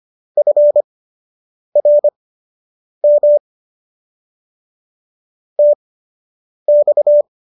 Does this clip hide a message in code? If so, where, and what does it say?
Yes: Morse code from 0.4 to 7.2 s, heard as a keyed tone, about 25 wpm, spaced as for 8 wpm, 590 Hz. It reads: FRM TX